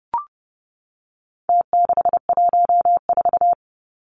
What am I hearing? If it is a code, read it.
Morse code, heard as a keyed tone, about 30 wpm, 700 Hz. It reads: T614
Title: Numbers - Contest Style - 4-Digit 30wpm